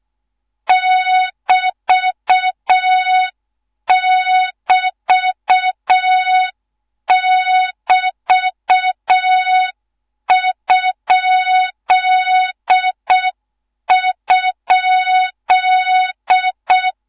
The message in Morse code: <BT><BT><BT>??